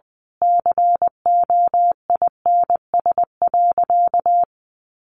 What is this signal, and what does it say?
Morse code, heard as a keyed tone, about 19 wpm, 700 Hz. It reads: COINS.
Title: A Sound Of Thunder 20wpm